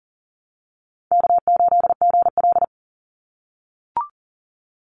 Morse code, heard as a keyed tone, about 40 wpm, 700 Hz. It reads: K8GL
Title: Easy Call Signs - 40wpm